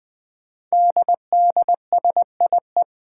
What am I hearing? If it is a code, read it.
Morse code, heard as a keyed tone, about 21 wpm, 700 Hz. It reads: DDSIE
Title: Five Challenging Letter-Number Combinations 20wpm